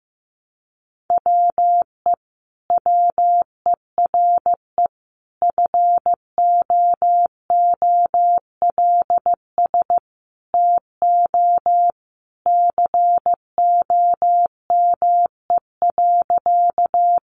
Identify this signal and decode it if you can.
Morse code, heard as a keyed tone, about 15 wpm, 700 Hz. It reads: WE WERE FOOLS TO COME.
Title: A Sound Of Thunder 15wpm